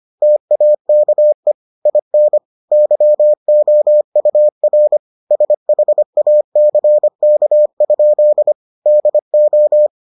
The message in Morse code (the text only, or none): TAKE IN YOUR SHACK? DO